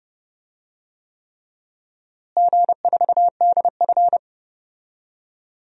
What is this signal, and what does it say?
Morse code, heard as a keyed tone, about 30 wpm, 700 Hz. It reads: G4DF
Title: DXCC-100 Call Signs - 4 Characters - Repeated 3x - 30wpm